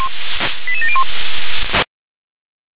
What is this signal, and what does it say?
Five-tone selective call (ZVEI), heard as quick runs of five tones, tones of about 70 ms. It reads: 88701
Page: /357/SDR/ZVEI/